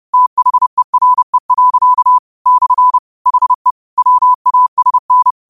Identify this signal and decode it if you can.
Morse code, heard as a keyed tone, about 30 wpm, 1000 Hz. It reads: THERE. / HE WASN